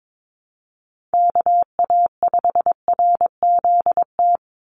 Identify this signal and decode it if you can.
Morse code, heard as a keyed tone, about 22 wpm, 700 Hz. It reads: KA5RZT
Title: USA Call Signs 22wpm